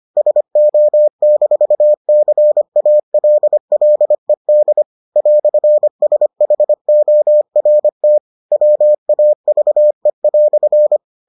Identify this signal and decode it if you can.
Morse code, heard as a keyed tone, about 25 wpm, 600 Hz. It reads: SO-CALLED "SHORT WAVE"